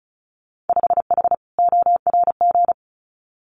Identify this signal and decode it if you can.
Morse code, heard as a keyed tone, about 34 wpm, 700 Hz. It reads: 5H ORG
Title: Winter Field Day Exchanges Encore 35wpm